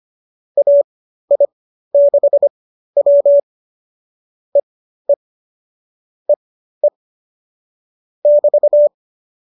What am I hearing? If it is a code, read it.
Morse code, heard as a keyed tone, about 25 wpm, spaced as for 13 wpm, 570 Hz. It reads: AI6W EE EE <BT>
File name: qso207 13